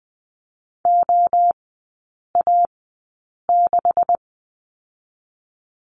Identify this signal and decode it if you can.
Morse code, heard as a keyed tone, about 20 wpm, spaced as for 8 wpm, 700 Hz. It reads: OA6
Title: 205 Lesson 33e - Letter j - Three Characters 20wpm